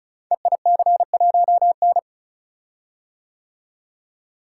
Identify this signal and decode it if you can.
Morse code, heard as a keyed tone, about 36 wpm, 700 Hz. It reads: EIC1N
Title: Code Groups - 5 Letter:Numbers - 35wpm